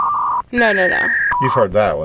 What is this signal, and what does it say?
Fax machine handshake, with speech.